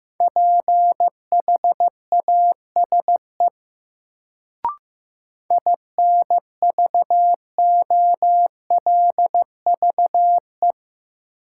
Morse code, heard as a keyed tone, about 15 wpm, 700 Hz. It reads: PHASE INVOLVE
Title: Sentences from Top 1200 Words - New Word Drill Alt - 15wpm